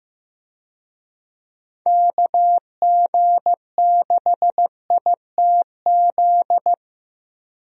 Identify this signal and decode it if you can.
Morse code, heard as a keyed tone, about 15 wpm, 700 Hz. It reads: KG6ITZ
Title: USA Call Signs - 6 Characters - Repeated 2x - 15wpm